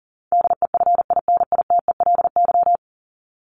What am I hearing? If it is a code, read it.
Morse code, heard as a keyed tone, about 40 wpm, 700 Hz. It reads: DEFINITELY
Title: Top 1751-2000 Words 40wpm